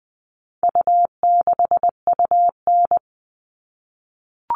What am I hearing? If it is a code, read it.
Morse code, heard as a keyed tone, about 20 wpm, 700 Hz. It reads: U6UN
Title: All Cut Numbers - 4-Digit 20wpm